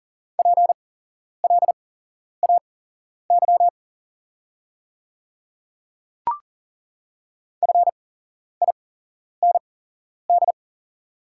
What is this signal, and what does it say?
Morse code, heard as a keyed tone, about 40 wpm, spaced as for 11 wpm, 690 Hz. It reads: PLAY FIND
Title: 8x Character Spacing - Top 300 Words - 40wpm